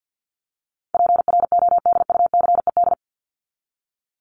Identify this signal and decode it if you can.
Morse code, heard as a keyed tone, about 52 wpm, 700 Hz. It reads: PRODUCED
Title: ED Suffix - Top 100 Words with Base Words 50wpm